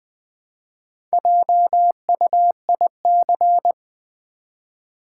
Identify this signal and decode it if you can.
Morse code, heard as a keyed tone, about 20 wpm, 700 Hz. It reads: JUIC
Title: Build A Word - 1 Letter At A Time - 5-Letter Words - 20wpm